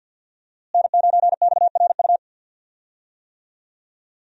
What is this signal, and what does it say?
Morse code, heard as a keyed tone, about 52 wpm, 700 Hz. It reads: N9XRU
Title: USA Call Signs - 5 Characters - Repeated 3x - 50wpm